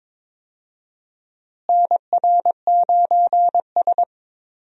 Morse code, heard as a keyed tone, about 21 wpm, 700 Hz. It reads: NR9S
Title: USA Call Signs - 4 Characters - Repeated 2x - 22wpm